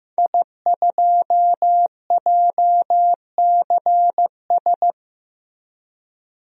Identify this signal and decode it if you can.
Morse code, heard as a keyed tone, about 16 wpm, 700 Hz. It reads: I2JCS
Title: DXCC-100 Call Signs - 5 Characters - Repeated 2x - 15wpm